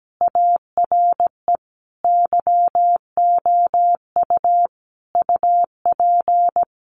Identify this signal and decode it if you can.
Morse code, heard as a keyed tone, about 17 wpm, 700 Hz. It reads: ARE YOU UP
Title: Sentences from Top 100 Words 17wpm